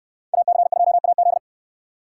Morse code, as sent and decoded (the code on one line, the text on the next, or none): .- -.-. ..--- .- --..
AC2AZ